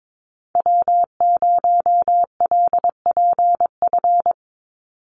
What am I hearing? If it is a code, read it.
Morse code, heard as a keyed tone, about 22 wpm, 700 Hz. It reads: W0LPF
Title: Call Signs Repeated 3x - 22wpm